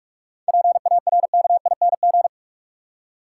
Morse code, heard as a keyed tone, about 46 wpm, 700 Hz. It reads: PARKING
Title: Seven Letter Words - 45wpm